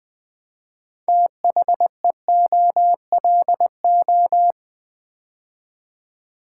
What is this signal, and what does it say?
Morse code, heard as a keyed tone, about 20 wpm, 700 Hz. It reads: THEOLO
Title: Build A Word - 2 Letters At A Time - 8-Letter Words - 20wpm